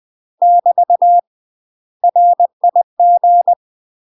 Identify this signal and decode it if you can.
Morse code, heard as a keyed tone, about 20 wpm, 700 Hz. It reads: <BT> RIG
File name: UK MASTER QSOs TRAINING / 20WPM / UK MASTER QSOs 0045